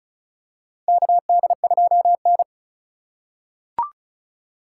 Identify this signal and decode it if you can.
Morse code, heard as a keyed tone, about 36 wpm, 700 Hz. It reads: KD2N
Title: USA Call Signs - 4 Characters - 3x Character Spacing - 35wpm